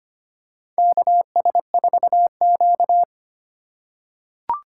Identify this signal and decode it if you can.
Morse code, heard as a keyed tone, about 25 wpm, 700 Hz. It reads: KS4Q
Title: USA Call Signs - 3x Character Spacing - 25wpm